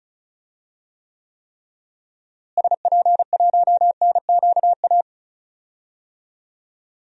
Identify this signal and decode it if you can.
Morse code, heard as a keyed tone, about 36 wpm, 700 Hz. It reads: SP1NQA
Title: DXCC-100 Call Signs - 6 Characters - Repeated 3x - 35wpm